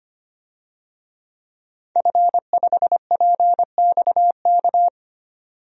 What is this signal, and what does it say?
Morse code, heard as a keyed tone, about 25 wpm, 700 Hz. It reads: F5PXK
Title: DXCC-100 Call Signs - 5 Characters - 25wpm